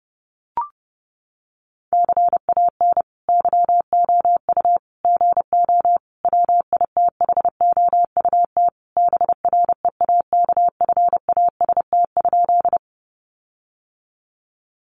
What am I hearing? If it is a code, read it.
Morse code, heard as a keyed tone, about 30 wpm, 700 Hz. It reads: CAN YOU GO WITHOUT BREAKFAST?
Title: Sentences from Top 1500 Words 30wpm